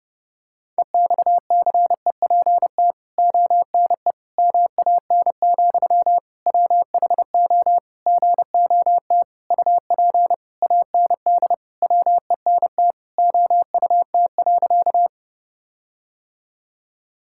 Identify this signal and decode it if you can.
Morse code, heard as a keyed tone, about 30 wpm, 700 Hz. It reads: EXCEPT ONE MAN, WHO GOT UP AND WENT OUT.
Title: Sentences from Top 1600 Words 30wpm